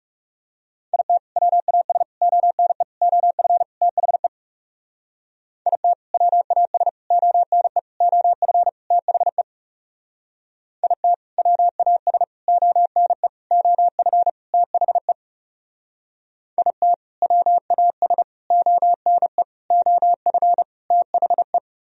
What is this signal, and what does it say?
Morse code, heard as a keyed tone, about 38 wpm, 700 Hz. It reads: IT WAS ONE OF THE IT WAS ONE OF THE IT WAS ONE OF THE IT WAS ONE OF THE